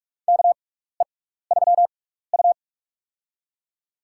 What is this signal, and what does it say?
Morse code, heard as a keyed tone, about 47 wpm, spaced as for 16 wpm, 700 Hz. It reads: KE3U
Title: USA Call Signs - 4 Characters - 6x Character Spacing - 45wpm